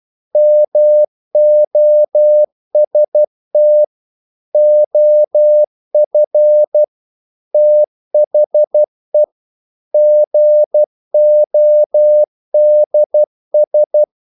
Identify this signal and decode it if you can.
Morse code, heard as a keyed tone, about 12 wpm, 600 Hz. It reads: MOST OF THE GODS
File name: CW / Practice / EnglishQuotations0112